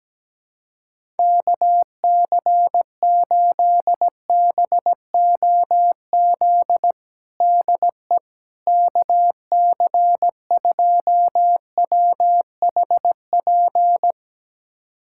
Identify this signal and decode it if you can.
Morse code, heard as a keyed tone, about 17 wpm, 700 Hz. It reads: KC8BOZ DE KC2WHP
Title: USA Callsign DE Callsign - 6 Characters - 17wpm